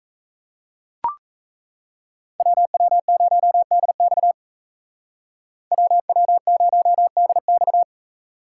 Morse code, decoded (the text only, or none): WW0DX WW0DX